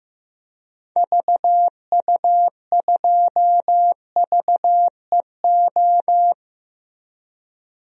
Morse code, heard as a keyed tone, about 15 wpm, 700 Hz. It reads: VU2VEO
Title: DXCC-100 Call Signs - 6 Characters - Repeated 2x - 15wpm